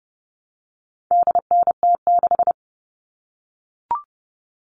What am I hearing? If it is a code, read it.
Morse code, heard as a keyed tone, about 30 wpm, 700 Hz. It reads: DNT6
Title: All Cut Numbers - 4-Digit 30wpm